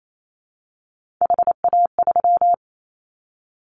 Morse code, heard as a keyed tone, about 28 wpm, 700 Hz. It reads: HA3
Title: Callsign Prefixes for Top 25 DXCC - Repeated 3x - 28wpm